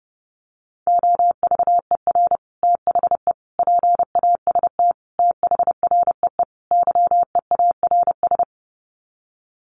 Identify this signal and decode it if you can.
Morse code, heard as a keyed tone, about 30 wpm, 700 Hz. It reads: OVER THE PAST THREE YEARS
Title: Sets of 5 Words Encore2 30wpm